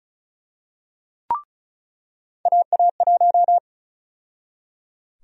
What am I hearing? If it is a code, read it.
Morse code, heard as a keyed tone, about 34 wpm, 700 Hz. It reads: AA1
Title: USA Call Sign Prefixes - 35wpm